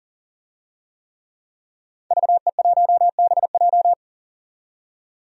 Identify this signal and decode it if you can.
Morse code, heard as a keyed tone, about 40 wpm, 700 Hz. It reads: VE1BJ
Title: DXCC-100 Call Signs - 5 Characters - 40wpm